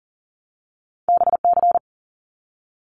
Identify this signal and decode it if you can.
Morse code, heard as a keyed tone, about 42 wpm, 700 Hz. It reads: BC
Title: Canadian Provinces and Territories 40wpm